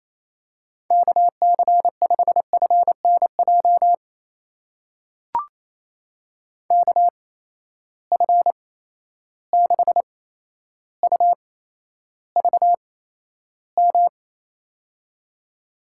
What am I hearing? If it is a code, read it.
Morse code, heard as a keyed tone, about 28 wpm, spaced as for 12 wpm, 700 Hz. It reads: KC5FNJ KF6UVM
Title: USA Call Signs - 6 Characters - 8x Character Spacing - 28wpm